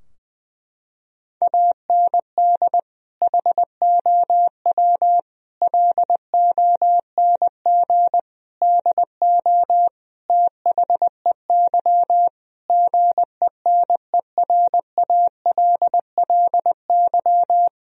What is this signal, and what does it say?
Morse code, heard as a keyed tone, about 20 wpm, 700 Hz. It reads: AND HOW LONG DO THEY GENERALLY
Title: Sentences from Top 1000 Words - Review Part 4 - 20wpm